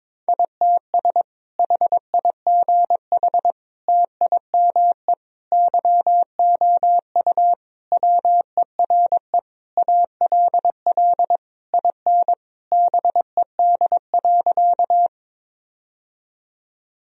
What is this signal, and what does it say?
Morse code, heard as a keyed tone, about 22 wpm, 700 Hz. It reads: ITS HIGH TIME YOU WERE ALL IN BED.
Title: Sentences from Top 1000 Words - Review Part 1 - 22wpm